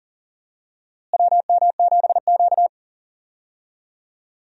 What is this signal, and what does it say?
Morse code, heard as a keyed tone, about 40 wpm, 700 Hz. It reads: WM7Q
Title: USA Call Signs - 4 Characters - Repeated 3x - 40wpm